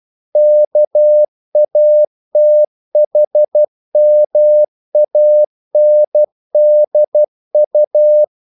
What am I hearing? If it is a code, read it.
Morse code, heard as a keyed tone, about 12 wpm, 600 Hz. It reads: KATHMANDU
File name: CW / Practice / NationalCapitals12